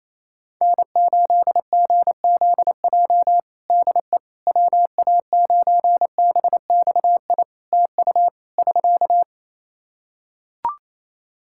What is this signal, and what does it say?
Morse code, heard as a keyed tone, about 27 wpm, 700 Hz. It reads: N8GZJ DE WA9BXI TU <SK>
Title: K1USN SST QSO Practice 28wpm